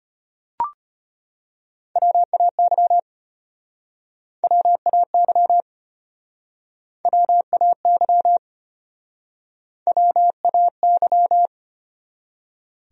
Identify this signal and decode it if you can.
Morse code, heard as a keyed tone, about 31 wpm, 700 Hz. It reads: WAY WAY WAY WAY
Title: Speed-Racing Top 200 Words 25wpm